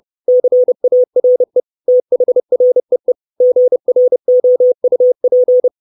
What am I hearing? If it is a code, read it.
Morse code, heard as a keyed tone, about 30 wpm, 500 Hz. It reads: CARE THREE GROUP
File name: 632 Words 30WPM